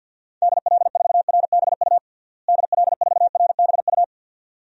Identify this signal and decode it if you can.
Morse code, heard as a keyed tone, about 45 wpm, 700 Hz. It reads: DL4RBU DL4RBU
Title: DXCC-100 Call Signs - 6 Characters - Repeated 2x - 50wpm